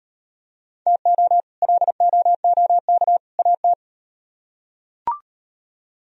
Morse code, heard as a keyed tone, about 38 wpm, 700 Hz. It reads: TO LOOK AT